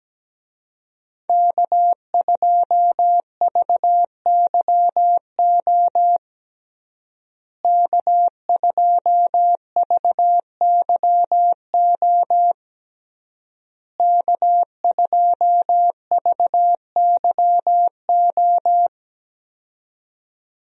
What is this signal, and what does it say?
Morse code, heard as a keyed tone, about 17 wpm, 700 Hz. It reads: K2VYO K2VYO K2VYO
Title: USA Call Signs - 5 Characters - Repeated 3x - 17wpm